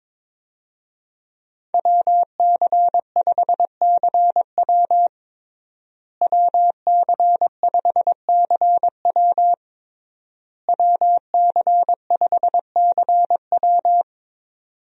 Morse code, heard as a keyed tone, about 22 wpm, 700 Hz. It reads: WC5CW WC5CW WC5CW
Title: USA Call Signs - 5 Characters - Repeated 3x - 22wpm